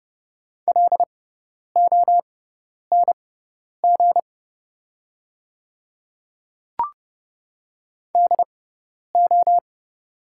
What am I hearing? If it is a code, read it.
Morse code, heard as a keyed tone, about 31 wpm, spaced as for 10 wpm, 700 Hz. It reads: LONG DO
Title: 6x Character Spacing - Top 100 Words - 30wpm